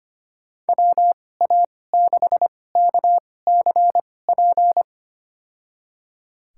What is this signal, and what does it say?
Morse code, heard as a keyed tone, about 25 wpm, spaced as for 18 wpm, 700 Hz. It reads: WA6KCP
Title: USA Call Signs - 6 Characters - 2x Character Spacing - 25wpm